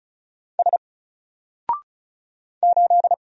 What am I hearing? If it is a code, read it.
Morse code, heard as a keyed tone, about 34 wpm, 700 Hz. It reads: S 8
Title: Single Character (Letter:Number:Punctuation:Prosign) 35wpm